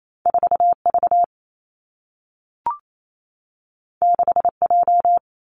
Morse code, heard as a keyed tone, about 28 wpm, 700 Hz. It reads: 4 V 6 J
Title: Two Challenging Letter-Number Combinations 28wpm